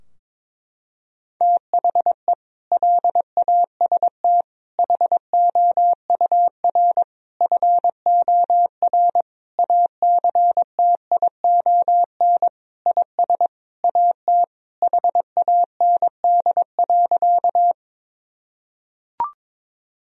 Morse code, heard as a keyed tone, about 22 wpm, 700 Hz. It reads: THE LAST HOUR FOR ACTION IS AT HAND.
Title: Sentences from Top 1000 Words - Review Part 4 - 22wpm